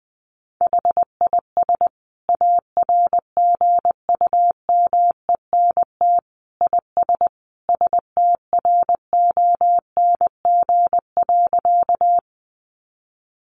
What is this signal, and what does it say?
Morse code, heard as a keyed tone, about 20 wpm, 700 Hz. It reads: HIS ARGUMENT IS STRONG.
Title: Sentences from Top 1400 Words 20wpm